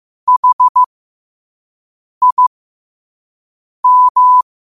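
Morse code, heard as a keyed tone, about 15 wpm, spaced as for 5 wpm, 1000 Hz. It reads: HIM
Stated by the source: Morse code quote of the day sent at 5 wpm.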